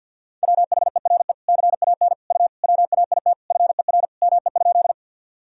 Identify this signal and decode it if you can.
Morse code, heard as a keyed tone, about 50 wpm, 700 Hz. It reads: WHERE CAN U WAIT FER ME?
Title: Sentences from Top 500 Words and Abbreviations 50wpm